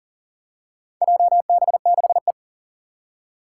Morse code, heard as a keyed tone, about 44 wpm, 700 Hz. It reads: JB6E
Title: Four Challenging Letter-Number Combinations 40wpm